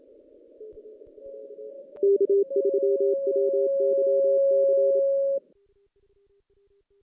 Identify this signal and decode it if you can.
Morse code, heard as a keyed tone, about 27 wpm, 400 Hz. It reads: K3WYC
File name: 2016 JAN CW NAQP / 2016 Jan naqp k5ot - 0-5